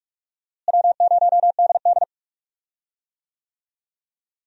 Morse code, heard as a keyed tone, about 46 wpm, 700 Hz. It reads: W0DD
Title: USA Call Signs - 4 Characters - Repeated 2x - 45wpm